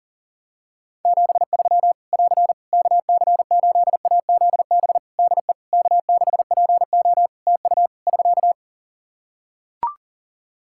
Morse code, heard as a keyed tone, about 37 wpm, 700 Hz. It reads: 73 <AR> KC8AZB DE K6PO TU <SK>
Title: Practice QSO - Exchange 4 - Ending - 40wpm